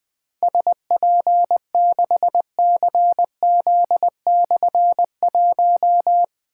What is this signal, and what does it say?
Morse code, heard as a keyed tone, about 20 wpm, 700 Hz. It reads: SP6CZ/1
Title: Difficult Call Signs 20wpm